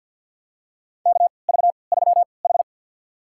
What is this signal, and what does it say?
Morse code, heard as a keyed tone, about 52 wpm, spaced as for 29 wpm, 700 Hz. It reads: KV3H
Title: USA Call Signs - 4 Characters - 3x Character Spacing - 50wpm